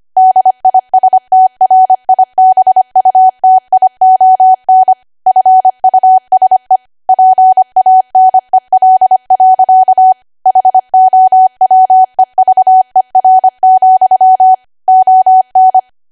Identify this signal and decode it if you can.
Morse code, heard as a keyed tone, about 25 wpm, 750 Hz. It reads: DISTRIBUTION FUSE PANEL. HOWEVER, ON
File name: Archive / 25 WPM / 25WPM